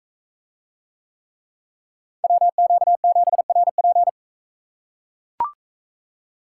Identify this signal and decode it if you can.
Morse code, heard as a keyed tone, about 38 wpm, 700 Hz. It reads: WQ7RP